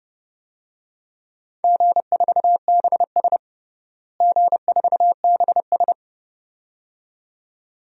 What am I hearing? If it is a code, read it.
Morse code, heard as a keyed tone, about 30 wpm, 700 Hz. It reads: G4BS G4BS
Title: DXCC-100 Call Signs - 4 Characters - Repeated 2x - 30wpm